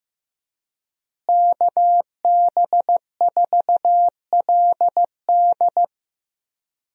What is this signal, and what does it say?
Morse code, heard as a keyed tone, about 14 wpm, 700 Hz. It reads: KB4LD
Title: USA Call Signs - 5 Characters - Repeated 2x - 15wpm